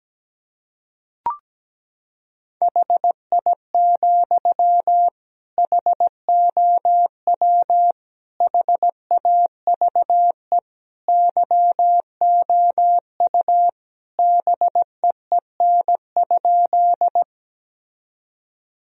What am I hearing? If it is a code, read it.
Morse code, heard as a keyed tone, about 17 wpm, 700 Hz. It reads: HI, HOW HAVE YOU BEEN?
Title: Sentences from Top 1000 Words - Review Part 3 - 17wpm